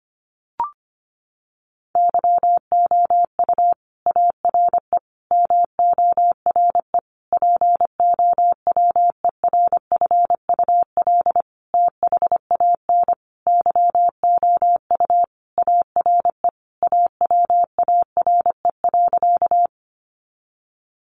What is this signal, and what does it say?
Morse code, heard as a keyed tone, about 25 wpm, 700 Hz. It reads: YOU ARE MORE POWERFUL THAN YOU ARE AWARE.
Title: Sentences from Top 1300 Words 25wpm